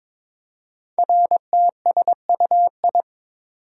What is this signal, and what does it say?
Morse code, heard as a keyed tone, about 22 wpm, 700 Hz. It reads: RTSUI
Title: Code Groups - 5 Letters - 22wpm